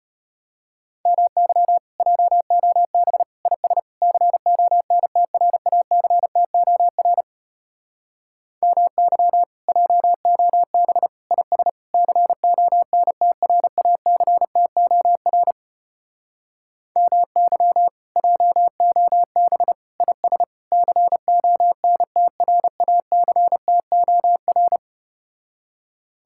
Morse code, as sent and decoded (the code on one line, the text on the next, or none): -- -.-- / .--- --- -... / .. ... / -.-. --- -. - .-. .- -.-. - --- .-. / -- -.-- / .--- --- -... / .. ... / -.-. --- -. - .-. .- -.-. - --- .-. / -- -.-- / .--- --- -... / .. ... / -.-. --- -. - .-. .- -.-. - --- .-.
MY JOB IS CONTRACTOR MY JOB IS CONTRACTOR MY JOB IS CONTRACTOR